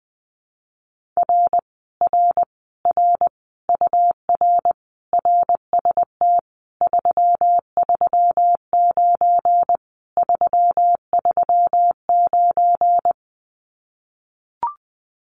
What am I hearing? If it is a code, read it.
Morse code, heard as a keyed tone, about 20 wpm, 700 Hz. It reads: R R R UR RST 339 339
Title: Common QSO Phrases Encore 20wpm